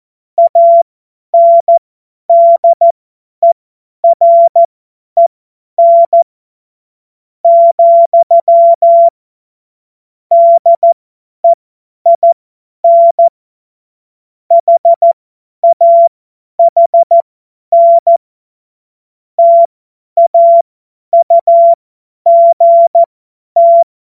Morse code, heard as a keyed tone, about 14 wpm, spaced as for 10 wpm, 680 Hz. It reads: ANDEREN , DEIN HAHN TAUGT
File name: Witz-29-Juli-2025-50-F70